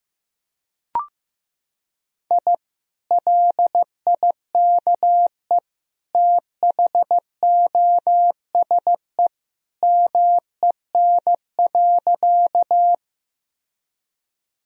Morse code, heard as a keyed tone, about 15 wpm, 700 Hz. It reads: I LIKE THOSE MEN.
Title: Sentences from Top 1000 Words - Review Part 1 - 15wpm